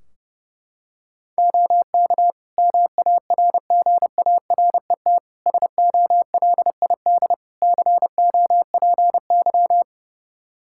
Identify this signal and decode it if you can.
Morse code, heard as a keyed tone, about 30 wpm, 700 Hz. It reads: OK MARGARET SOLID COPY
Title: Practice QSO - Exchange 3 - More Info - 30wpm